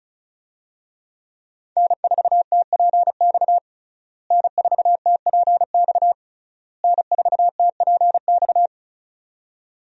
Morse code, heard as a keyed tone, about 36 wpm, 700 Hz. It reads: N4TPX N4TPX N4TPX
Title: USA Call Signs - 5 Characters - Repeated 3x - 35wpm